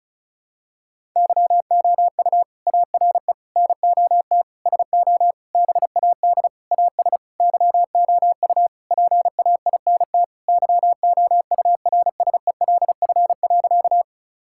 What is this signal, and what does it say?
Morse code, heard as a keyed tone, about 34 wpm, 700 Hz. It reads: YOU ARE NOT SO BAD AS YOU PAINT YOURSELF.
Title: Sentences from Top 1000 Words - Review Part 6 - 35wpm